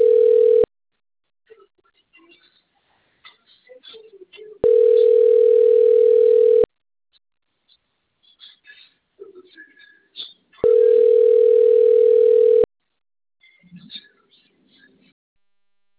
Ringback tone (North America), 440 + 480 Hz.
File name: out- - - - - .12509